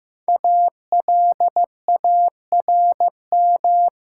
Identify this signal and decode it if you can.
Morse code, heard as a keyed tone, about 15 wpm, 700 Hz. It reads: ALARM
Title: Five Letter Words - 15wpm